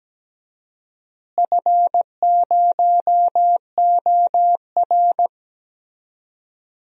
Morse code, heard as a keyed tone, about 17 wpm, 700 Hz. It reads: F0OR
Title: DXCC-100 Call Signs - 4 Characters - Repeated 2x - 17wpm